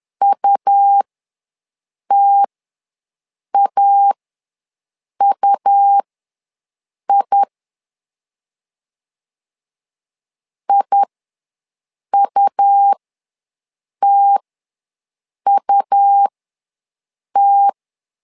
Morse code, heard as a keyed tone, about 11 wpm, spaced as for 6 wpm, 800 Hz. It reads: UTAUI IUTUT
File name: LEZIONI / 2 UHDN / 2 U ws 10wpm 2017